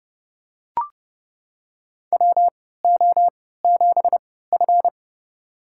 Morse code, heard as a keyed tone, about 31 wpm, spaced as for 17 wpm, 700 Hz. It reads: WO7F
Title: USA Call Signs - 4 Characters - 3x Character Spacing - 30wpm